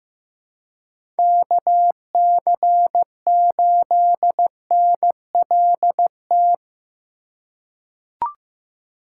Morse code, heard as a keyed tone, about 15 wpm, 700 Hz. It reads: KC8NLT
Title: USA Call Signs - 6 Characters - 3x Character Spacing - 15wpm